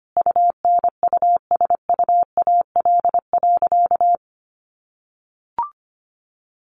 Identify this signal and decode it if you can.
Morse code, heard as a keyed tone, about 25 wpm, 700 Hz. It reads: UNUSUAL.
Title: Sentences from Top 1400 Words 25wpm